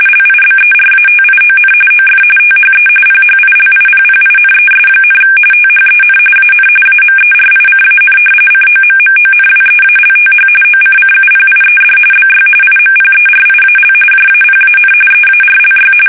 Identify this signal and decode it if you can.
RTTY, about 45.45 baud. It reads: LIBERATION OF THE PHILIPPINES, THEN AN AMERICAN COMMONWEALTH OCCUPIED BY IMPERIAL JAPAN EARLY
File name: fsk850 / battleshipnj